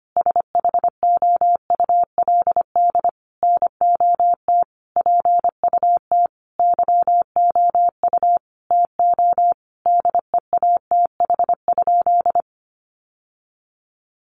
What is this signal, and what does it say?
Morse code, heard as a keyed tone, about 25 wpm, 700 Hz. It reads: SHOULD NOT PUT YOU TO DEATH?
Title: Sentences from Top 1100 Words 25wpm